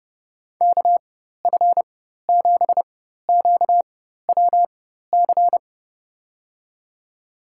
Morse code, heard as a keyed tone, about 30 wpm, spaced as for 14 wpm, 700 Hz. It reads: KF7QWC
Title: USA Call Signs - 6 Characters - 4x Character Spacing - 30wpm